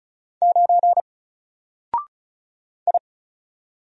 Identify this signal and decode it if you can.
Morse code, heard as a keyed tone, about 36 wpm, 700 Hz. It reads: 9 I
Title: Single Character (Letter:Number:Punctuation:Prosign) Alt 35wpm